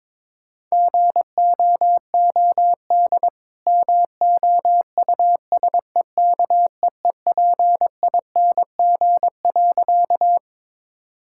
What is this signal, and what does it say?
Morse code, heard as a keyed tone, about 21 wpm, 690 Hz. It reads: GOOD MOUSEKEEPING.
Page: morse-code-ninja-practice-22wpm